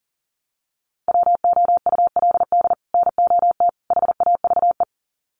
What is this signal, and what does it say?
Morse code, heard as a keyed tone, about 40 wpm, 700 Hz. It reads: WOULD NOT HAVE
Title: Sets of 3 Words 40wpm